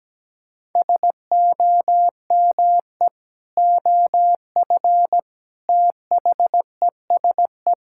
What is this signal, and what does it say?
Morse code, heard as a keyed tone, about 17 wpm, 700 Hz. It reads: SOME OF THESE
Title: Sentences from Top 1000 Words - Review Part 6 - 17wpm